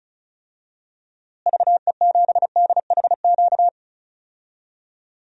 Morse code, heard as a keyed tone, about 34 wpm, 700 Hz. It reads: VE7DHQ
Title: DXCC-100 Call Signs - 6 Characters - Repeated 3x - 35wpm